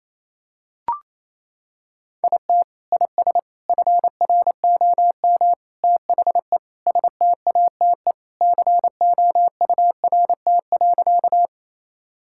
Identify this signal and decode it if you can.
Morse code, heard as a keyed tone, about 28 wpm, 700 Hz. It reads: IT IS FROM THE STATE COURT.
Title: Sentences from Top 1600 Words 28wpm